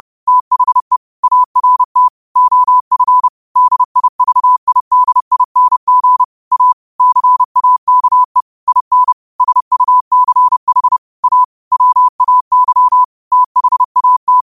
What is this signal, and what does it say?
Morse code, heard as a keyed tone, about 30 wpm, 1000 Hz. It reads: THE ART OF DIVIDING A CAKE IN SUCH A WAY THAT
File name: qotdfiles / 30wpm